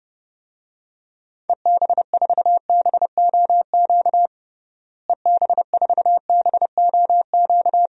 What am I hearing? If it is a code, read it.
Morse code, heard as a keyed tone, about 30 wpm, 700 Hz. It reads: EB4BOQ EB4BOQ
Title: DXCC-100 Call Signs - 6 Characters - Repeated 3x - 30wpm